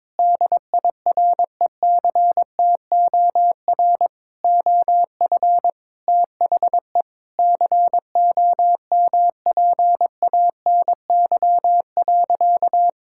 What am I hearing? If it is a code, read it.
Morse code, heard as a keyed tone, about 22 wpm, 700 Hz. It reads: DIRECTOR OF THE COMPANY.
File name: practice / 22wpm / Sentences with 3-Word Phrases - 22wpm